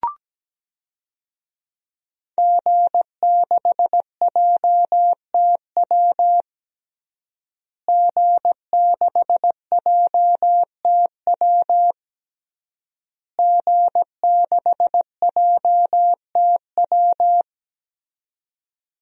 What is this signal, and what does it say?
Morse code, heard as a keyed tone, about 17 wpm, 700 Hz. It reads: G6JTW G6JTW G6JTW
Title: DXCC-100 Call Signs - 5 Characters - Repeated 3x - 17wpm